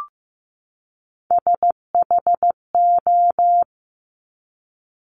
Morse code, heard as a keyed tone, about 15 wpm, 700 Hz. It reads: SHO
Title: Top Three Letter Combinations in Words 15wpm